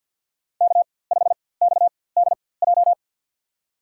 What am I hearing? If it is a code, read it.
Morse code, heard as a keyed tone, about 52 wpm, spaced as for 24 wpm, 690 Hz. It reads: K5XDJ
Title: USA Call Signs - 5 Characters - 4x Character Spacing - 50wpm